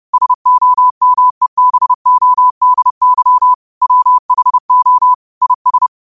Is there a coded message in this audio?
Yes: Morse code from 0.1 to 5.9 s, heard as a keyed tone, about 30 wpm, 1000 Hz. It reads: SOMEBODY WHO IS